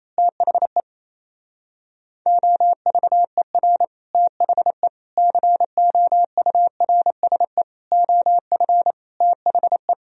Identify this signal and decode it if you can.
Morse code, heard as a keyed tone, about 26 wpm, 700 Hz. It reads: THE OVER THE COURSE OF THE
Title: Speed-Racing - Sets of 5 Words 28wpm